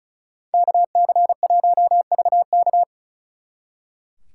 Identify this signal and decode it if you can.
Morse code, heard as a keyed tone, about 45 wpm, 700 Hz. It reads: KC1VK